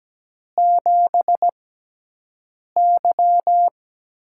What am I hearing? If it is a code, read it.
Morse code, heard as a keyed tone, about 17 wpm, spaced as for 6 wpm, 700 Hz. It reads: 7Y
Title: USA Call Signs - 4 Characters - 6x Character Spacing - 17wpm